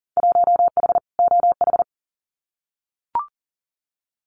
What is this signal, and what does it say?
Morse code, heard as a keyed tone, about 39 wpm, 700 Hz. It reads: 1H OH
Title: Winter Field Day Exchanges 40wpm